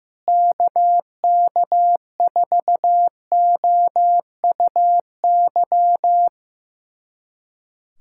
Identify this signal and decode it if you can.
Morse code, heard as a keyed tone, about 15 wpm, 700 Hz. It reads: KK4OUY
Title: USA Callsign DE Callsign 15wpm